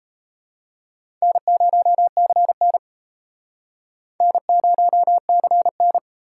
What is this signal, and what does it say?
Morse code, heard as a keyed tone, about 32 wpm, 700 Hz. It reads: N0CN N0CN
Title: Speed Racing - USA 1x2 Call Signs 28wpm